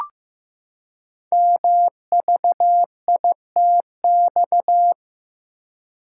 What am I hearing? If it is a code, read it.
Morse code, heard as a keyed tone, about 15 wpm, 700 Hz. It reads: MVITX
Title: Code Groups - 5 Letters - 15wpm